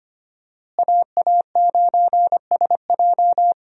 Morse code, heard as a keyed tone, about 25 wpm, 700 Hz. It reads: AA9SJ
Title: USA Call Signs - 5 Characters - Repeated 3x - 25wpm